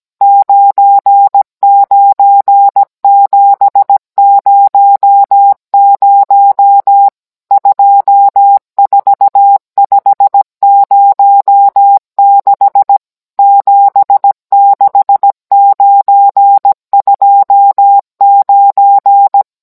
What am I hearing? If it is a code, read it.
Morse code, heard as a keyed tone, about 17 wpm, 800 Hz. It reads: 99700 24506 76929